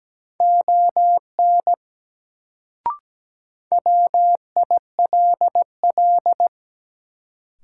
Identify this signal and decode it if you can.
Morse code, heard as a keyed tone, about 17 wpm, 700 Hz. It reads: ON WILL
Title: Sentences from Top 100 Words - New Word Drill - 17wpm